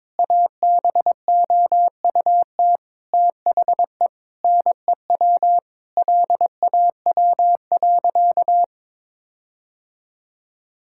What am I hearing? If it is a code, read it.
Morse code, heard as a keyed tone, about 22 wpm, 700 Hz. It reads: ABOUT THE NEW LAW.
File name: practice / 22wpm / Sentences with 8-Word Phrases - 22wpm